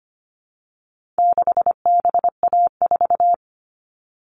Morse code, heard as a keyed tone, about 25 wpm, 700 Hz. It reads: 6BA4
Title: All Cut Numbers - 4-Digit 25wpm